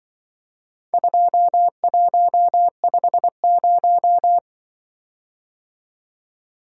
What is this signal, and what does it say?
Morse code, heard as a keyed tone, about 23 wpm, 700 Hz. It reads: 2150
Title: Speed-Racing - Four Letter-Number Combinations - 20wpm